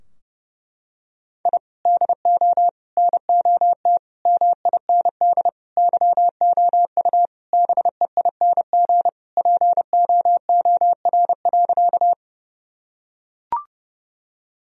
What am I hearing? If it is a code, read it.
Morse code, heard as a keyed tone, about 30 wpm, 700 Hz. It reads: I DO NOT MIND YOU BEING POOR.
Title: Sentences from Top 1000 Words - Review Part 3 - 30wpm